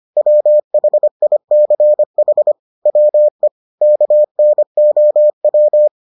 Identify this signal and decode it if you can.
Morse code, heard as a keyed tone, about 25 wpm, 600 Hz. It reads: WHICH WE KNOW